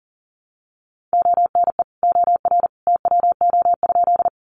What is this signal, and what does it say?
Morse code, heard as a keyed tone, about 40 wpm, 700 Hz. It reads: ONE OR TWO?
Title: Sentences from Top 100 Words 40wpm